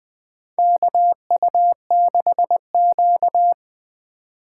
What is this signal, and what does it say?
Morse code, heard as a keyed tone, about 20 wpm, 700 Hz. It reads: KU6Q
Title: USA Call Signs - 4 Characters - 4x Character Spacing - 20wpm